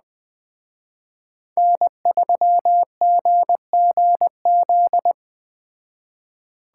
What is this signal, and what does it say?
Morse code, heard as a keyed tone, about 20 wpm, 700 Hz. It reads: N3GGZ
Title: Speed Racing - USA 1x3 Call Signs 17wpm